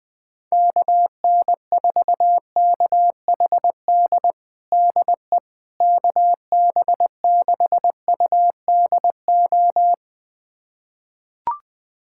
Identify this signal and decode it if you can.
Morse code, heard as a keyed tone, about 20 wpm, 700 Hz. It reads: KN4KHD DE KB6UDO
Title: USA Callsign DE Callsign 20wpm